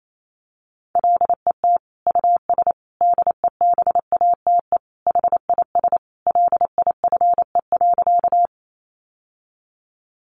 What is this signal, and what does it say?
Morse code, heard as a keyed tone, about 28 wpm, 700 Hz. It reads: LET US DEBATE HIS LIFE.
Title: Sentences from Top 1100 Words 28wpm